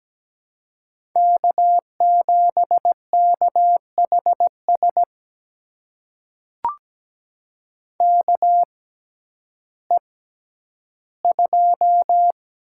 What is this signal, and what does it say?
Morse code, heard as a keyed tone, about 17 wpm, spaced as for 10 wpm, 700 Hz. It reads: K7KHS KE2
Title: USA Call Signs - 5 Characters - 6x Character Spacing - 17wpm